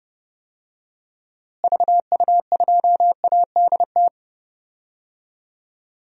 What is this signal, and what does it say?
Morse code, heard as a keyed tone, about 30 wpm, 700 Hz. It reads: VU2ADT
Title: DXCC-100 Call Signs - 6 Characters - Repeated 2x - 30wpm